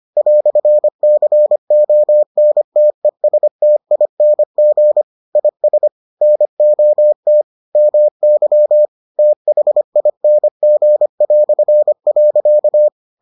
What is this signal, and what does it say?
Morse code, heard as a keyed tone, about 25 wpm, 600 Hz. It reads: "CONTESTING IS NOT MY THING".